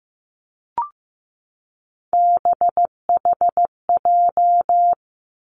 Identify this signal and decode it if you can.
Morse code, heard as a keyed tone, about 15 wpm, 700 Hz. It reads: BHJ
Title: Three Challenging Letter-Number Combinations 15wpm